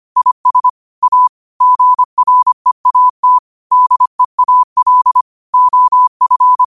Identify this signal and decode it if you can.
Morse code, heard as a keyed tone, about 25 wpm, 1000 Hz. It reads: IS A GREAT DEAL OF
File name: qotdfiles / 25wpm